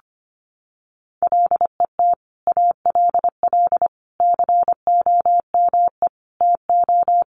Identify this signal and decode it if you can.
Morse code, heard as a keyed tone, about 25 wpm, 700 Hz. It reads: LET ALL COME TO
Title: Sentences from Top 1100 Words 25wpm